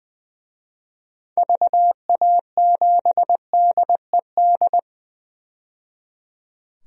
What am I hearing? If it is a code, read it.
Morse code, heard as a keyed tone, about 20 wpm, 700 Hz. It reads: VA7DED
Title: DXCC-100 Call Signs - 6 Characters - 20wpm